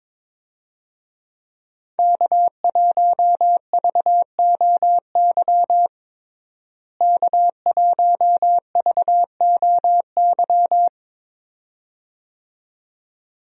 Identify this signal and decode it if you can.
Morse code, heard as a keyed tone, about 22 wpm, 700 Hz. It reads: K1VOY K1VOY
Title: USA Call Signs - 5 Characters - Repeated 2x - 22wpm